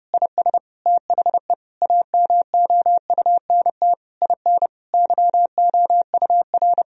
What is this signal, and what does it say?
Morse code, heard as a keyed tone, about 32 wpm, 700 Hz. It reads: IS THE AMOUNT IN YOUR
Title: Sentences from Top 1000 Words - Review Part 4 - 30wpm